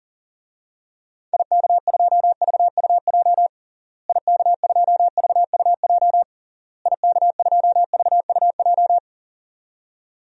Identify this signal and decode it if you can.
Morse code, heard as a keyed tone, about 40 wpm, 700 Hz. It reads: IK2VUJ IK2VUJ IK2VUJ
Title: DXCC-100 Call Signs - 6 Characters - Repeated 3x - 40wpm